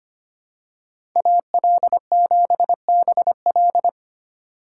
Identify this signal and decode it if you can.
Morse code, heard as a keyed tone, about 25 wpm, 700 Hz. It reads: AL7BL
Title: USA Call Signs - 5 Characters - 4x Character Spacing - 25wpm